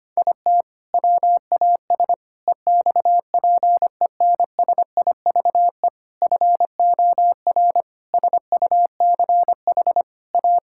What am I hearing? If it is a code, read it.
Morse code, heard as a keyed tone, about 25 wpm, 700 Hz. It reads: IT WAS EXPENSIVE FOR SUCH A
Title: Sentences from Top 1000 Words - Review Part 6 - 25wpm